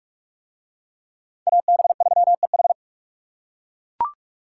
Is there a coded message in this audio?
Yes: Morse code from 1.5 to 2.7 s, heard as a keyed tone, about 39 wpm, 700 Hz. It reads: AB3EH